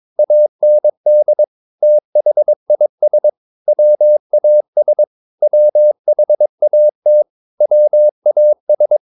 Morse code, heard as a keyed tone, about 22 wpm, 600 Hz. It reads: AND THIS WAS WHAT WAS
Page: CW med Gnister